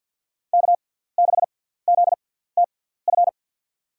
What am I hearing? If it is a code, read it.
Morse code, heard as a keyed tone, about 52 wpm, spaced as for 17 wpm, 700 Hz. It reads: K6ZTF
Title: USA Call Signs - 5 Characters - 6x Character Spacing - 50wpm